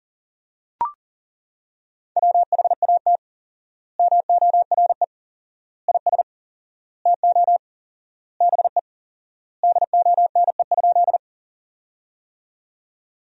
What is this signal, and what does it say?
Morse code, heard as a keyed tone, about 40 wpm, 700 Hz. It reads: WHAT MORE IS TO BE DONE?
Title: 4x Word-Spacing - Sentences from Top 500 Words - 40wpm